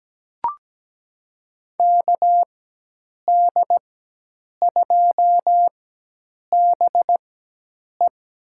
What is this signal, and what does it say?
Morse code, heard as a keyed tone, about 17 wpm, spaced as for 8 wpm, 700 Hz. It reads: KD2BE
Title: USA Call Signs - 5 Characters - 4x Character Spacing - 17wpm